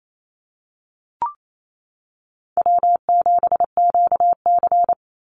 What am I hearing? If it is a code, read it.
Morse code, heard as a keyed tone, about 28 wpm, 700 Hz. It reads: W7QC
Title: Easy Call Signs Repeated 2x - 28wpm